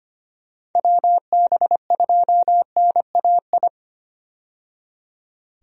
Morse code, heard as a keyed tone, about 25 wpm, 700 Hz. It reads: WB2NAI
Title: USA Call Signs - 6 Characters - 25wpm